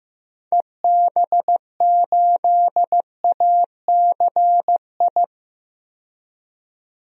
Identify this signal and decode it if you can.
Morse code, heard as a keyed tone, about 15 wpm, 700 Hz. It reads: EB8ACI
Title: DXCC-100 Call Signs - 6 Characters - 15wpm